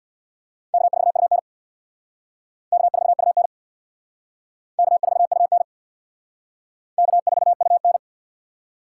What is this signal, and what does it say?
Morse code, heard as a keyed tone, about 64 wpm, 690 Hz. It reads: K4UN K4UN K4UN K4UN